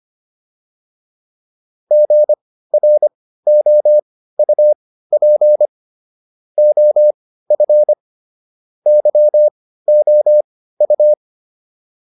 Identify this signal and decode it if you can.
Morse code, heard as a keyed tone, about 25 wpm, spaced as for 15 wpm, 600 Hz. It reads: GROUP OF YOU